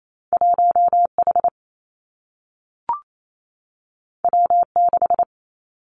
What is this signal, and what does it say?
Morse code, heard as a keyed tone, about 28 wpm, 700 Hz. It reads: 1 H W 6
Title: Two Challenging Letter-Number Combinations 28wpm